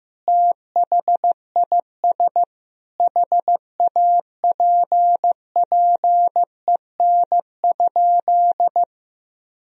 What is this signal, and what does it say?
Morse code, heard as a keyed tone, about 15 wpm, 700 Hz. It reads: THIS HAPPEN?
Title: Sentences from Top 400 Words and Abbreviations 15wpm